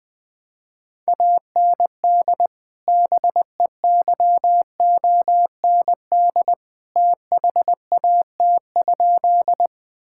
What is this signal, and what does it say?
Morse code, heard as a keyed tone, about 20 wpm, 700 Hz. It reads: AND BEYOND THAT?
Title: Sentences from Top 1000 Words - Review Part 6 - 20wpm